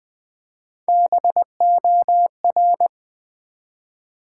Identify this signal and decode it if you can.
Morse code, heard as a keyed tone, about 20 wpm, 700 Hz. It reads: BOR
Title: Build A Word - 1 Letter At A Time - 3 to 5-Letter Words - 20wpm